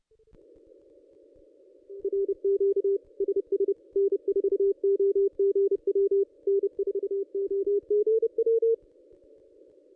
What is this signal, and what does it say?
Morse code, heard as a keyed tone, about 30 wpm, 440 Hz. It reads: RQ SS N4OGW N4OGW
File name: ss-cw-k5tr-op-n5rz